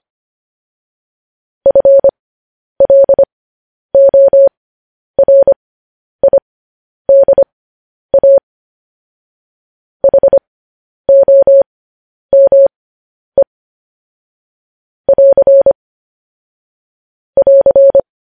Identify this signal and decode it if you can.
Morse code, heard as a keyed tone, about 25 wpm, spaced as for 10 wpm, 550 Hz. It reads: FLORIDA HOME <AR> <AR>